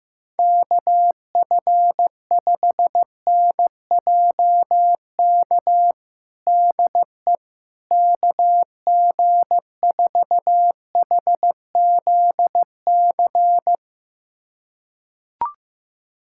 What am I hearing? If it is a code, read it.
Morse code, heard as a keyed tone, about 16 wpm, 700 Hz. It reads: KF5NJK DE KG4HZC
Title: USA Callsign DE Callsign - 6 Characters - 15wpm